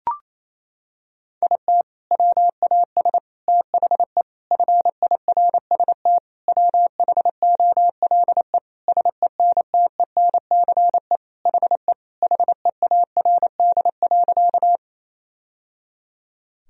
Morse code, heard as a keyed tone, about 28 wpm, 700 Hz. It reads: IT WAS THE FIRST WHOLE SENTENCE HE HEARD.
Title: Sentences from Top 1000 Words - Review Part 1 - 28wpm